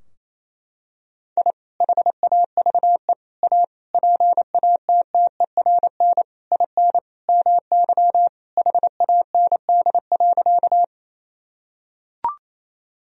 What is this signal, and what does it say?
Morse code, heard as a keyed tone, about 28 wpm, 700 Hz. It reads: I HAVE A PATTERN IN MY HAND.
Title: Sentences from Top 1000 Words - Review Part 2 - 28wpm